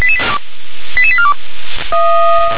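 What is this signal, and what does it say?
Five-tone selective call (ZVEI), heard as quick runs of five tones, tones of about 70 ms. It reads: 88842
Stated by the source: ZVEI five-tone selective call received by SDR.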